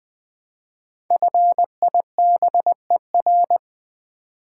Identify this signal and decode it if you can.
Morse code, heard as a keyed tone, about 20 wpm, 690 Hz. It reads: FIBER
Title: 5 Letter Words - First 3 Letters Given - 20wpm